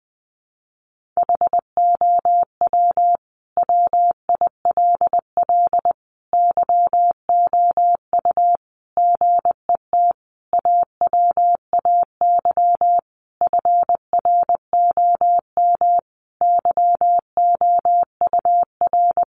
Morse code, heard as a keyed tone, about 20 wpm, 700 Hz. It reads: HOW WILL YOU GET AWAY FROM YOUR
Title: Sentences from Top 1500 Words 20wpm